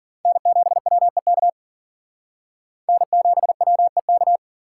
Morse code, heard as a keyed tone, about 48 wpm, 700 Hz. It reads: N7WEK N7WEK
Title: Speed Racing - USA 1x3 Call Signs 40wpm